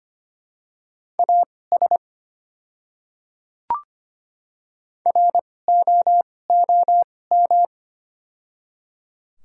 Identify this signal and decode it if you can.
Morse code, heard as a keyed tone, about 25 wpm, spaced as for 18 wpm, 700 Hz. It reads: AS ROOM
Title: 2x Character Spacing - Top 300 Words - 25wpm